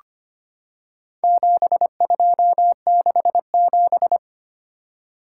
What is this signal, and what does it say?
Morse code, heard as a keyed tone, about 25 wpm, 700 Hz. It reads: 7267
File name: practice / 25wpm / Four Letter-Number Combinations - 25wpm